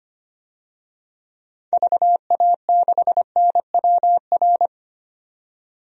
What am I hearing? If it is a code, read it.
Morse code, heard as a keyed tone, about 25 wpm, 700 Hz. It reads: VA6NWR
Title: DXCC-100 Call Signs - 6 Characters - 25wpm